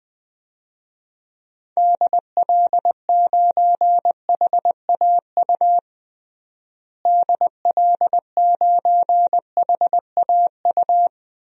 Morse code, heard as a keyed tone, about 20 wpm, 700 Hz. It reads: DL9HAU DL9HAU
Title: DXCC-100 Call Signs - 6 Characters - Repeated 3x - 20wpm